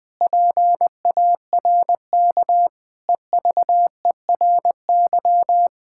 Morse code, heard as a keyed tone, about 20 wpm, 700 Hz. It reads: PARK EVERY
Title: Sentences with 2-Word Phrases - 20wpm